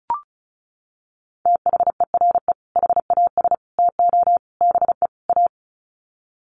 Morse code, heard as a keyed tone, about 34 wpm, 700 Hz. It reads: THERE HAS TO BE A
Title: Sets of 5 Words Encore 35wpm